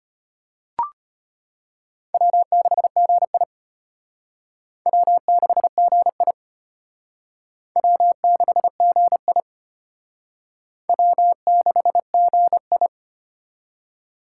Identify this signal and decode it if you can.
Morse code, heard as a keyed tone, about 31 wpm, 700 Hz. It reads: W6GI W6GI W6GI W6GI
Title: Speed Racing - USA 1x2 Call Signs 25wpm